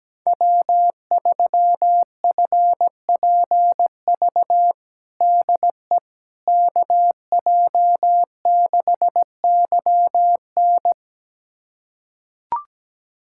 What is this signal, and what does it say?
Morse code, heard as a keyed tone, about 17 wpm, 700 Hz. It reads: W3FPV DE KJ6YN
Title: USA Callsign DE Callsign - 5 Characters - 17wpm